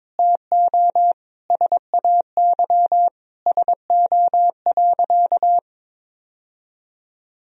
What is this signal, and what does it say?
Morse code, heard as a keyed tone, about 22 wpm, 700 Hz. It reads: TO SAY SO.
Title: Sentences from Top 1000 Words - Review Part 1 - 22wpm